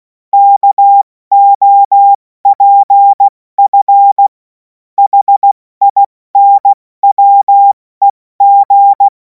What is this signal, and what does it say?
Morse code, heard as a keyed tone, about 16 wpm, spaced as for 14 wpm, 800 Hz. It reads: KOPF HINWEG
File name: Morsekurs / Witze / Witz 17. Februar 2026 70 F80